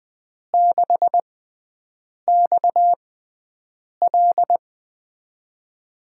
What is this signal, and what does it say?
Morse code, heard as a keyed tone, about 20 wpm, spaced as for 7 wpm, 700 Hz. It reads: 6XL
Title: USA Call Signs - 5 Characters - 6x Character Spacing - 20wpm